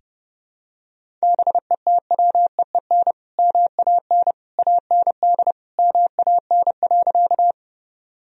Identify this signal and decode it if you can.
Morse code, heard as a keyed tone, about 30 wpm, 700 Hz. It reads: BETWEEN MAN AND MAN.
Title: Sentences from Top 1000 Words - Review Part 1 - 30wpm